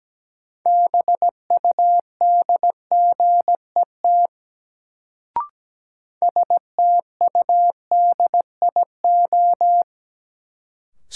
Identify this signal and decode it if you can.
Morse code, heard as a keyed tone, about 17 wpm, 700 Hz. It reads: BUDGET STUDIO
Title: Sentences from Top 1200 Words - New Word Drill Alt - 17wpm